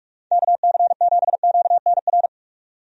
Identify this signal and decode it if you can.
Morse code, heard as a keyed tone, about 41 wpm, 700 Hz. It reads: KC7QNR
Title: USA Call Signs - 6 Characters - 45wpm